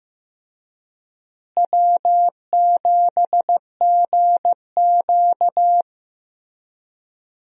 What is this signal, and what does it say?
Morse code, heard as a keyed tone, about 16 wpm, 700 Hz. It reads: W7GQ
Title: USA Call Signs - 4 Characters - Repeated 3x - 15wpm